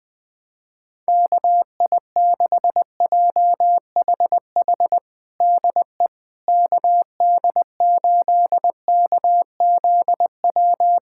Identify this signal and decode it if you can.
Morse code, heard as a keyed tone, about 20 wpm, 700 Hz. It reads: KI6JHH DE KD8KZW
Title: USA Callsign DE Callsign 20wpm